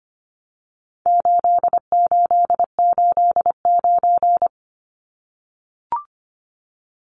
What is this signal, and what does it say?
Morse code, heard as a keyed tone, about 25 wpm, 700 Hz. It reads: 8889
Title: Numbers - 4-Digit 25wpm